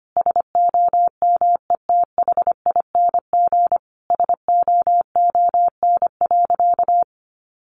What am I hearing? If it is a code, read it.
Morse code, heard as a keyed tone, about 25 wpm, 700 Hz. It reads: SOMETHING SOON.
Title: Sentences from Top 1500 Words 25wpm